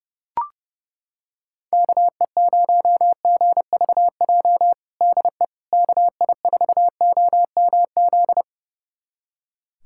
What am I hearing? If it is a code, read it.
Morse code, heard as a keyed tone, about 32 wpm, 700 Hz. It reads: KE0GVJ DE KI4OMZ
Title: USA Callsign DE Callsign 30wpm